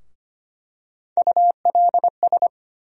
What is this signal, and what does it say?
Morse code, heard as a keyed tone, about 25 wpm, 700 Hz. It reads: ULS
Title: USA Call Sign Postfixes - 3 Letters 25wpm